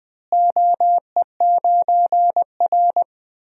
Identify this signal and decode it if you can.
Morse code, heard as a keyed tone, about 20 wpm, 700 Hz. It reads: OE9R
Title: DXCC-100 Call Signs - 4 Characters - Repeated 3x - 20wpm